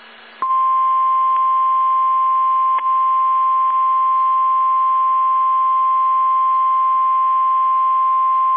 Emergency Alert System (SAME) tones (the 1050 Hz attention tone). It is heard from 1.0 to 8.2 s.